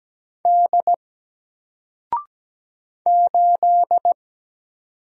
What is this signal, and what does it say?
Morse code, heard as a keyed tone, about 17 wpm, 700 Hz. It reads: D 8
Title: Single Character (Letter:Number:Punctuation:Prosign) 17wpm